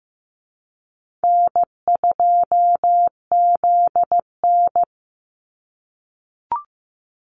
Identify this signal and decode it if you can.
Morse code, heard as a keyed tone, about 15 wpm, 700 Hz. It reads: N2ZN
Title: Easy Call Signs - 15wpm